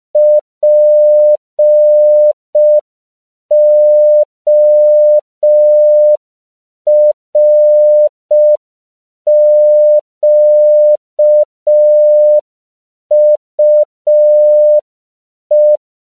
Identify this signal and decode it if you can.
Morse code, heard as a keyed tone, about 5 wpm, 600 Hz. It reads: PORQUE
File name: CW / 5PPM